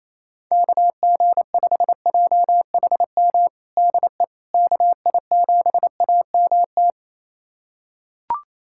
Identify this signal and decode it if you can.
Morse code, heard as a keyed tone, about 28 wpm, 700 Hz. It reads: KG5JHM DE KI7AMT
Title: USA Callsign DE Callsign - 6 Characters - 28wpm